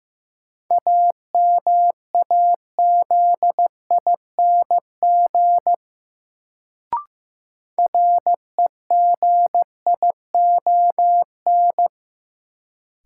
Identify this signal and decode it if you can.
Morse code, heard as a keyed tone, about 15 wpm, 700 Hz. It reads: AMAZING REGION
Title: Sentences from Top 1500 Words - New Word Drill Alt - 15wpm